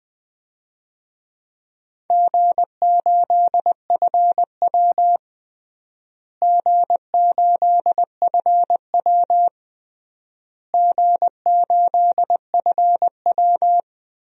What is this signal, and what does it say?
Morse code, heard as a keyed tone, about 20 wpm, 700 Hz. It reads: G8FW G8FW G8FW
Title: DXCC-100 Call Signs - 4 Characters - Repeated 3x - 20wpm